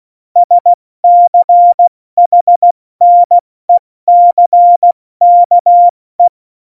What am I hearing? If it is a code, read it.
Morse code, heard as a keyed tone, about 16 wpm, 700 Hz. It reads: SCHNECKE
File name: Morsekurs / Witze / Witz 27 March 2018, 70 F80